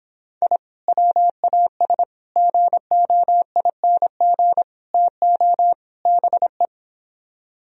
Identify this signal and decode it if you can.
Morse code, heard as a keyed tone, about 29 wpm, 700 Hz. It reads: I WAS GOING TO BE